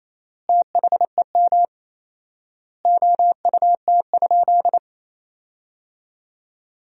Morse code, heard as a keyed tone, about 28 wpm, 700 Hz. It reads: THEM OUT?
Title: 4x Word-Spacing - Sentences from Top 300 Words - 28wpm